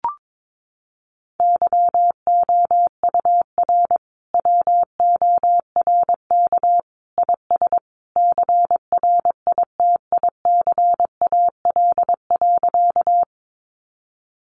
Morse code, heard as a keyed tone, about 22 wpm, 700 Hz. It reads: YOUR WORK IS CRITICAL.
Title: Sentences from Top 1100 Words 22wpm